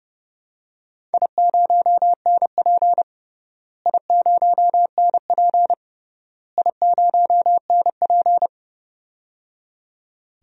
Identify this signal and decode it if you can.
Morse code, heard as a keyed tone, about 30 wpm, 700 Hz. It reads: I0NP I0NP I0NP
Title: DXCC-100 Call Signs - 4 Characters - Repeated 3x - 30wpm